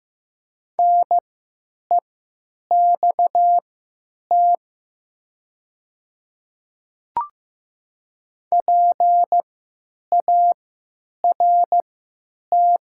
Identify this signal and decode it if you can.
Morse code, heard as a keyed tone, about 15 wpm, spaced as for 8 wpm, 700 Hz. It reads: NEXT PART
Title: 3x Character Spacing - Top 300 Words - 15wpm